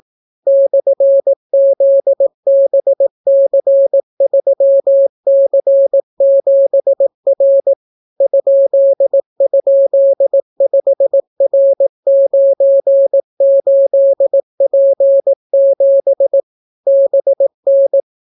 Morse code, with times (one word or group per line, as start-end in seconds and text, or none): 0.5-7.7 /ZBC3C7R
8.2-16.4 ??5R98P7
16.9-18.0 BN